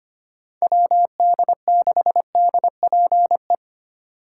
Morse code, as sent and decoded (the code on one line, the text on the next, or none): .-- -.. -.... -.. .--. .
WD6DPE